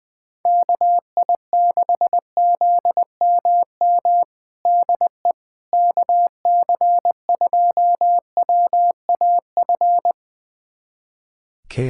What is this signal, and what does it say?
Morse code, heard as a keyed tone, about 20 wpm, 700 Hz. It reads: KI6ZMM DE KC2WAF
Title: USA Callsign DE Callsign 20wpm